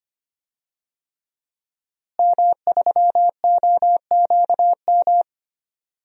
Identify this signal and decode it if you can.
Morse code, heard as a keyed tone, about 25 wpm, 700 Hz. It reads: M3OQM
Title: DXCC-100 Call Signs - 5 Characters - Repeated 3x - 25wpm